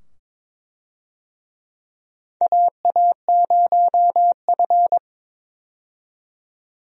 Morse code, heard as a keyed tone, about 22 wpm, 700 Hz. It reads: AA0F
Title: USA Call Signs - 4 Characters - 22wpm